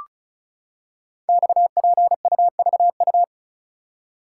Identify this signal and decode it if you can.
Morse code, heard as a keyed tone, about 36 wpm, 700 Hz. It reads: XPUVU
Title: Code Groups - 5 Letters - 35wpm